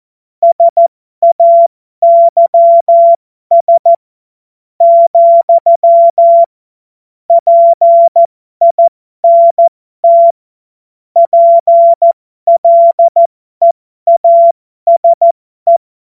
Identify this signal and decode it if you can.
Morse code, heard as a keyed tone, about 14 wpm, spaced as for 11 wpm, 680 Hz. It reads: SAYS, PINT PLEASE
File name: Morsekurs / Witze